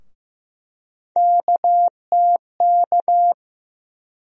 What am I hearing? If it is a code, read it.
Morse code, heard as a keyed tone, about 15 wpm, 700 Hz. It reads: KTK
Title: USA Call Sign Postfixes - 3 Letters 15wpm